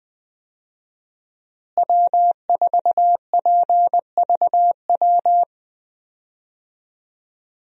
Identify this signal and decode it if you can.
Morse code, heard as a keyed tone, about 20 wpm, 700 Hz. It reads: W4PVW
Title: USA Call Signs - 5 Characters - Repeated 3x - 20wpm